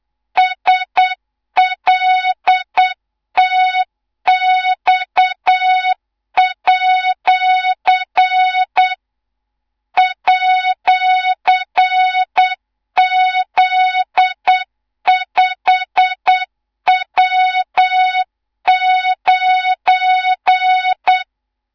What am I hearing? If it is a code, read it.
Morse code, heard as a keyed tone, about 8 wpm, 3700 Hz, with noise. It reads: SLTX@ @Z5W9
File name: Morze-HA2MN / hang / 18 IRASJELEK2